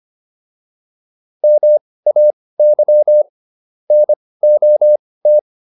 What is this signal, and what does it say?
Morse code, heard as a keyed tone, about 25 wpm, spaced as for 18 wpm, 600 Hz. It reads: MAY NOT